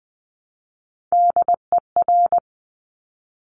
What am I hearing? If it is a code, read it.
Morse code, heard as a keyed tone, about 20 wpm, 700 Hz. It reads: DER
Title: Top Three Letter Combinations in Words 20wpm